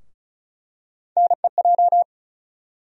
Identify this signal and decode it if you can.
Morse code, heard as a keyed tone, about 34 wpm, 700 Hz. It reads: NEJ
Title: USA Call Sign Postfixes - 3 Letters 35wpm